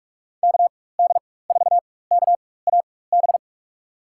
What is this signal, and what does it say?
Morse code, heard as a keyed tone, about 43 wpm, spaced as for 21 wpm, 700 Hz. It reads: KD4KAB
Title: USA Call Signs - 4x Character Spacing - 45wpm